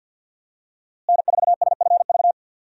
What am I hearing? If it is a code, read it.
Morse code, heard as a keyed tone, about 52 wpm, 700 Hz. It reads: N4SFV
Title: USA Call Signs - 5 Characters - Repeated 2x - 50wpm